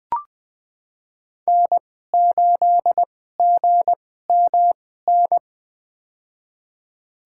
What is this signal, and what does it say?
Morse code, heard as a keyed tone, about 21 wpm, spaced as for 15 wpm, 700 Hz. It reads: N8GMN
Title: USA Call Signs - 5 Characters - 2x Character Spacing - 20wpm